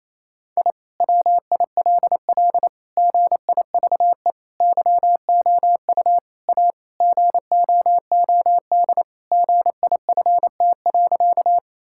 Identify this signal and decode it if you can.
Morse code, heard as a keyed tone, about 28 wpm, 700 Hz. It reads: I WILL GIVE YOU A GOOD GIFT.
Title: Sentences from Top 1000 Words - Review Part 6 - 28wpm